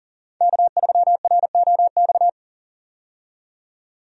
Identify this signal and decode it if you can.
Morse code, heard as a keyed tone, about 40 wpm, 700 Hz. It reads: K3ROX
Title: USA Call Signs - 5 Characters - 40wpm